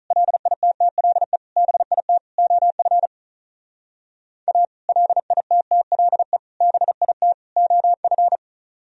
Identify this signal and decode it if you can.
Morse code, heard as a keyed tone, about 45 wpm, 700 Hz. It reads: LITTLE BIT OF A LITTLE BIT OF